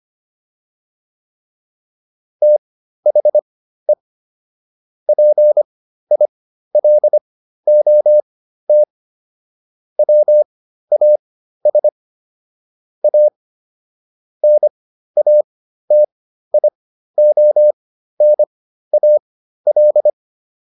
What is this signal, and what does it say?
Morse code, heard as a keyed tone, about 25 wpm, spaced as for 13 wpm, 600 Hz. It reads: THE PILOT WAS A NATIONAL